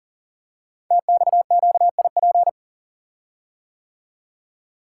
Morse code, heard as a keyed tone, about 42 wpm, 690 Hz. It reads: TXQIP
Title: Code Groups - 5 Letters - 40wpm